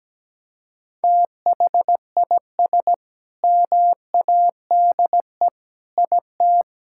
Morse code, heard as a keyed tone, about 17 wpm, 700 Hz. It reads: THIS MADE IT
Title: Sentences from Top 1600 Words 17wpm